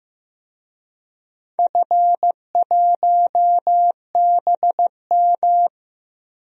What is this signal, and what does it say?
Morse code, heard as a keyed tone, about 16 wpm, 700 Hz. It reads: F1BM
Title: DXCC-100 Call Signs - 4 Characters - Repeated 3x - 15wpm